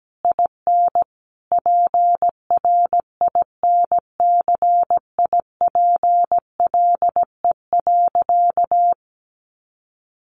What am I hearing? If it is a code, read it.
Morse code, heard as a keyed tone, about 17 wpm, 700 Hz. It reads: IN PRINCIPLE.
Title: Sentences from Top 1400 Words 17wpm